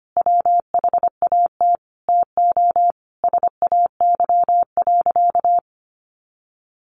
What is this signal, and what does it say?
Morse code, heard as a keyed tone, about 25 wpm, 700 Hz. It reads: WHAT TO SAY.
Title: Sentences from Top 1400 Words 25wpm